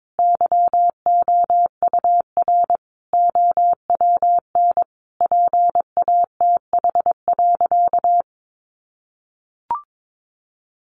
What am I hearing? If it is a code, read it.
Morse code, heard as a keyed tone, about 22 wpm, 700 Hz. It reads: YOUR OWN PATH.
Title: Sentences from Top 1400 Words 22wpm